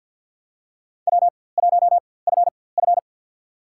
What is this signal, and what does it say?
Morse code, heard as a keyed tone, about 50 wpm, spaced as for 23 wpm, 690 Hz. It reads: W1FF
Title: USA Call Signs - 4x Character Spacing - 50wpm